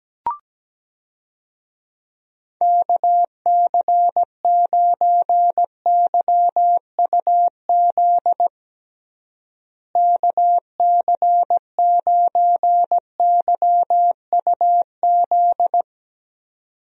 Morse code, heard as a keyed tone, about 17 wpm, 700 Hz. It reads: KC9YUZ KC9YUZ
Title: USA Call Signs - 6 Characters - Repeated 2x - 17wpm